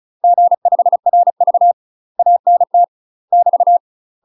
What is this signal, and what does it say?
Morse code, heard as a keyed tone, about 36 wpm, 700 Hz. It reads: G5RV ANT <BT>
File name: UK MASTER QSOs TRAINING / 35WPM / UK MASTER QSOs 0018